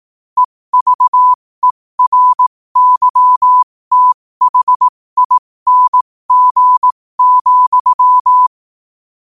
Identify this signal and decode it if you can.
Morse code, heard as a keyed tone, about 18 wpm, spaced as for 15 wpm, 1000 Hz. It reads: EVERYTHING,
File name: qotdfiles / 15wpm